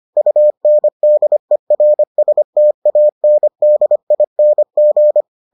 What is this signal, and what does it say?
Morse code, heard as a keyed tone, about 25 wpm, 600 Hz. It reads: UNDERSTANDING